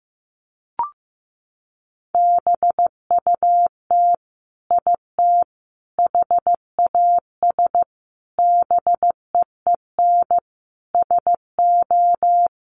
Morse code, heard as a keyed tone, about 15 wpm, 700 Hz. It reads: BUT IT HAS BEEN SO
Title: Sentences from Top 100 Words 15wpm